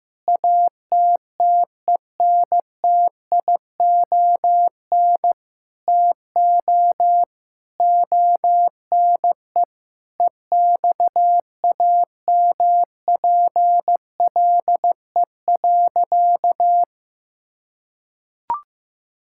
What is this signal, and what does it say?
Morse code, heard as a keyed tone, about 16 wpm, 700 Hz. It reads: ATTENTION TO ONE EXAMPLE.
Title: Sentences from Top 1000 Words - Review Part 6 - 15wpm